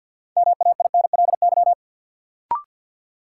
Morse code, heard as a keyed tone, about 52 wpm, 700 Hz. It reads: MAINLY
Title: Sentences from Top 1400 Words - New Word Drill Alt - 50wpm